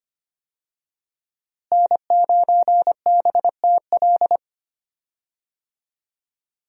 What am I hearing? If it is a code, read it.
Morse code, heard as a keyed tone, about 25 wpm, 700 Hz. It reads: N9BTL
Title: USA Call Signs - 5 Characters - Repeated 3x - 25wpm